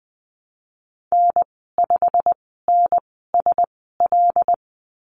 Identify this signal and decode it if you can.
Morse code, heard as a keyed tone, about 20 wpm, spaced as for 15 wpm, 700 Hz. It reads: N5NSL
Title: 147 Lesson 24g - Letter y - Call Signs 20wpm